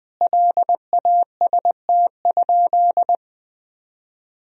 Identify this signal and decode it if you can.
Morse code, heard as a keyed tone, about 20 wpm, 700 Hz. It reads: LAST?
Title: Sentences from Top 1000 Words - Review Part 4 - 20wpm